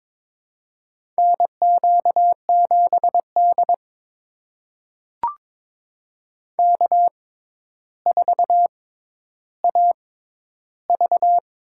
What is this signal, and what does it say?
Morse code, heard as a keyed tone, about 22 wpm, spaced as for 11 wpm, 700 Hz. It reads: NQ7D K4AV
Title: USA Call Signs - 4 Characters - 6x Character Spacing - 22wpm